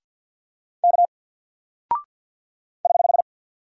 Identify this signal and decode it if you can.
Morse code, heard as a keyed tone, about 49 wpm, 700 Hz. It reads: K <HH>
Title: Single Character (Letter:Number:Punctuation:Prosign) Alt 50wpm